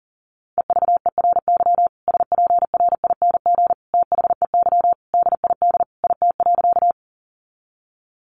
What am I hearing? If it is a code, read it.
Morse code, heard as a keyed tone, about 40 wpm, 700 Hz. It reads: EVERY SPRING THEY DID IT.
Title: Sentences from Top 1200 Words 40wpm